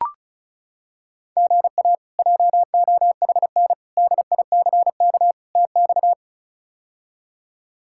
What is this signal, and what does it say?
Morse code, heard as a keyed tone, about 36 wpm, 700 Hz. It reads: GA JOHN DICK TX
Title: K1USN SST QSO Practice - 35wpm